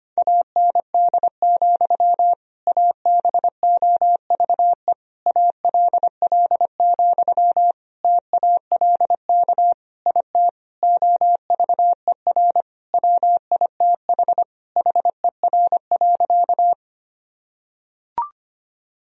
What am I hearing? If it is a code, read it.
Morse code, heard as a keyed tone, about 25 wpm, 700 Hz. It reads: AND, ABOVE ALL, TALK IT OVER WITH HER.
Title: Sentences from Top 1000 Words - Review Part 2 - 25wpm